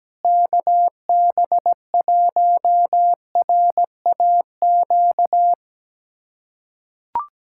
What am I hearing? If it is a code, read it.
Morse code, heard as a keyed tone, about 17 wpm, 700 Hz. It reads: KB1RAQ
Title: USA Callsign DE Callsign 17wpm